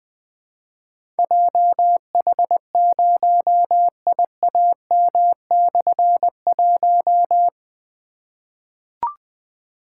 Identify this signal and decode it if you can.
Morse code, heard as a keyed tone, about 21 wpm, 700 Hz. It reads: JH0IAM/1
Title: Difficult Call Signs 20wpm